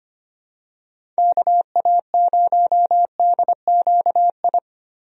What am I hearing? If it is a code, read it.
Morse code, heard as a keyed tone, about 25 wpm, 700 Hz. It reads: KA0DQI
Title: USA Call Signs - 6 Characters - Repeated 2x - 25wpm